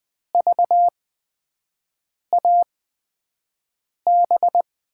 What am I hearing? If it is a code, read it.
Morse code, heard as a keyed tone, about 20 wpm, spaced as for 5 wpm, 700 Hz. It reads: VAB
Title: USA Call Signs - 8x Character Spacing - 20wpm